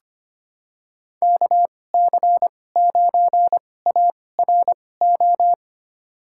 Morse code, heard as a keyed tone, about 25 wpm, spaced as for 18 wpm, 700 Hz. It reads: KC9ARO
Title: USA Call Signs - 6 Characters - 2x Character Spacing - 25wpm